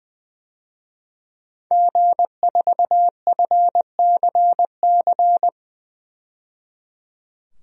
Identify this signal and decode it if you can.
Morse code, heard as a keyed tone, about 21 wpm, 700 Hz. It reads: G4FCC
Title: DXCC-100 Call Signs - 5 Characters - 20wpm